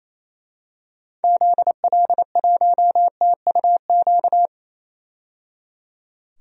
Morse code, heard as a keyed tone, about 28 wpm, 700 Hz. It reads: ZL1TUQ
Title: DXCC-100 Call Signs - 6 Characters - 28wpm